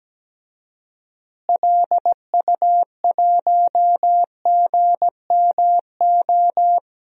Morse code, heard as a keyed tone, about 17 wpm, 700 Hz. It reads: LU1GMO
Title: DXCC-100 Call Signs - 6 Characters - Repeated 2x - 17wpm